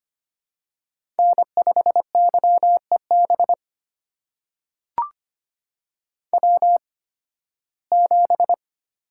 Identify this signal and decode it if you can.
Morse code, heard as a keyed tone, about 25 wpm, spaced as for 16 wpm, 700 Hz. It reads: N5YEB W7
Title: USA Call Signs - 5 Characters - 8x Character Spacing - 25wpm